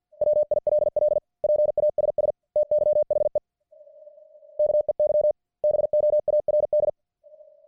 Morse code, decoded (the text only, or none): WILL PASS TYHE KEY BOARD